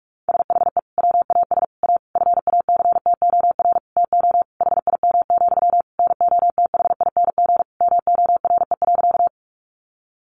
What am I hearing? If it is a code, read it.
Morse code, heard as a keyed tone, about 44 wpm, 700 Hz. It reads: SHE WAS A FACTOR TO HIM, NOTHING MORE.
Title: Sentences from Top 1300 Words 45wpm